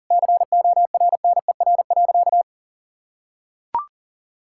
Morse code, heard as a keyed tone, about 37 wpm, 700 Hz. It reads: CORNER.
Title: Sentences from Top 1600 Words 40wpm